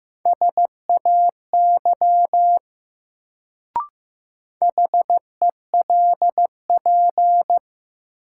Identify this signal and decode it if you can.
Morse code, heard as a keyed tone, about 15 wpm, 700 Hz. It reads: SAY HELP
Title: Sentences from Top 200 Words - New Word Drill - 15wpm